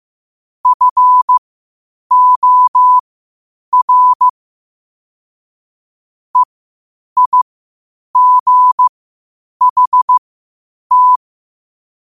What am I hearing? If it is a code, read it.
Morse code, heard as a keyed tone, about 15 wpm, spaced as for 8 wpm, 1000 Hz. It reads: FOR EIGHT
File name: qotdfiles / 8wpm